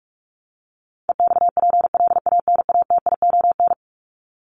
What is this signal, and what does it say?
Morse code, heard as a keyed tone, about 44 wpm, 700 Hz. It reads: EXPLANATION
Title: Top 1751-2000 Words 45wpm